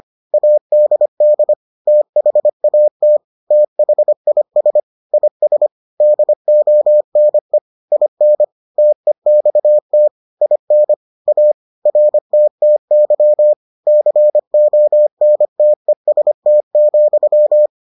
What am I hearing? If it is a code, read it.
Morse code, heard as a keyed tone, about 25 wpm, 600 Hz. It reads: ADD THAT THIS IS DONE IN TEXT IN A RTTY CONTEST,